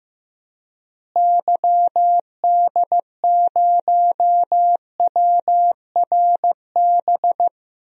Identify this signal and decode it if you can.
Morse code, heard as a keyed tone, about 15 wpm, 700 Hz. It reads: YD0WRB
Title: DXCC-100 Call Signs - 6 Characters - Repeated 3x - 15wpm